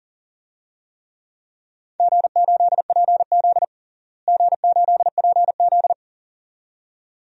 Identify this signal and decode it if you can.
Morse code, heard as a keyed tone, about 40 wpm, 700 Hz. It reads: G8PZ G8PZ
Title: DXCC-100 Call Signs - 4 Characters - Repeated 2x - 40wpm